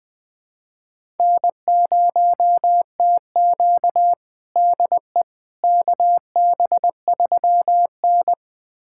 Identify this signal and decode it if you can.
Morse code, heard as a keyed tone, about 20 wpm, 700 Hz. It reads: N0TQ DE KB3N
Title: USA Callsign DE Callsign - 4 Characters - 20wpm